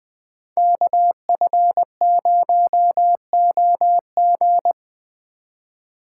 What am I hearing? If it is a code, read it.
Morse code, heard as a keyed tone, about 20 wpm, 700 Hz. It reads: KF0OG
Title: Speed Racing - USA 2x2 Call Signs 17wpm